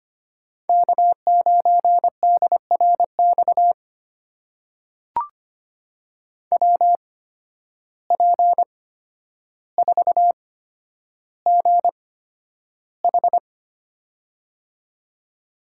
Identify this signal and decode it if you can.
Morse code, heard as a keyed tone, about 25 wpm, spaced as for 11 wpm, 700 Hz. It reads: K9DRX WP4GH
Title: USA Call Signs - 8x Character Spacing - 25wpm